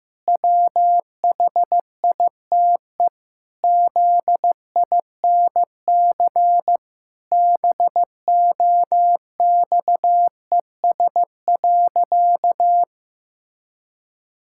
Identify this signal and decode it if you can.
Morse code, heard as a keyed tone, about 15 wpm, 700 Hz. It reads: WHITE ZINC BOXES.
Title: Pangram Challenge v2 - 15wpm